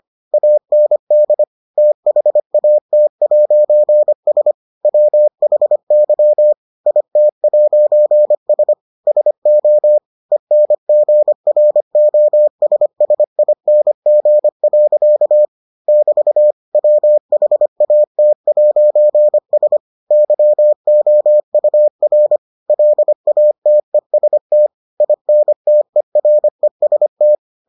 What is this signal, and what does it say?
Morse code, heard as a keyed tone, about 25 wpm, 600 Hz. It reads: AND THAT'S WHY IT'S SO ENGROSSING. <BT> WHAT'S YOUR LATEST INTEREST